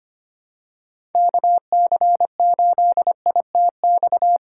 Morse code, heard as a keyed tone, about 25 wpm, 700 Hz. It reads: KC8ITX
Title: USA Call Signs - 6 Characters - 2x Character Spacing - 25wpm